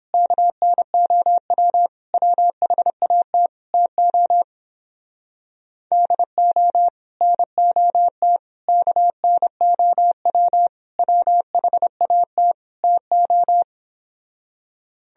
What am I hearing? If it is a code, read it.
Morse code, heard as a keyed tone, about 29 wpm, 700 Hz. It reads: KNOW WHAT TO DO NOT KNOW WHAT TO